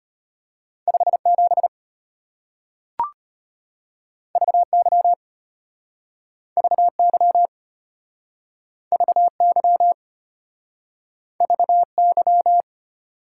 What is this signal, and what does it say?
Morse code, heard as a keyed tone, about 34 wpm, 700 Hz. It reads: 57 VY VY VY VY